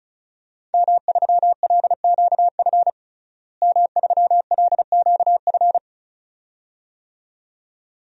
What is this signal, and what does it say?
Morse code, heard as a keyed tone, about 35 wpm, 700 Hz. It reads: M3LQF M3LQF
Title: DXCC-100 Call Signs - 5 Characters - Repeated 2x - 35wpm